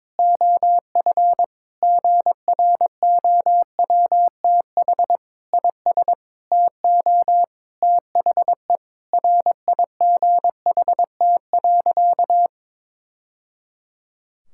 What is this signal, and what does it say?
Morse code, heard as a keyed tone, about 22 wpm, 700 Hz. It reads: OF GROWTH IS TO THE RIGHT.
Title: Sentences from Top 1000 Words - Review Part 5 - 22wpm